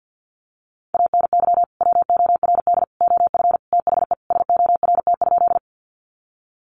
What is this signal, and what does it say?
Morse code, heard as a keyed tone, about 50 wpm, 700 Hz. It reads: ANY WORD OF THE SORT?
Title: Sentences from Top 1300 Words 50wpm